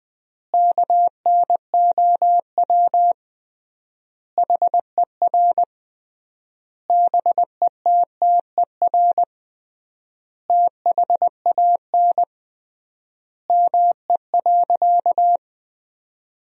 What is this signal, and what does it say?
Morse code, heard as a keyed tone, about 21 wpm, 700 Hz. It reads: KNOW HER BETTER THAN ME.
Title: 3x Word-Spacing - Sentences from Top 400 Words - 20wpm